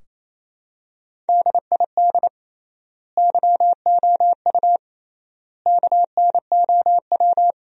Morse code, heard as a keyed tone, about 28 wpm, 700 Hz. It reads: DID YOU KNOW
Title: 3x Word-Spacing - Sentences from Top 100 Words - 28wpm